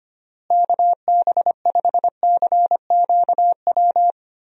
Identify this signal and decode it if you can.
Morse code, heard as a keyed tone, about 25 wpm, 700 Hz. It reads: KB5CQW
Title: USA Call Signs - 6 Characters - Repeated 2x - 25wpm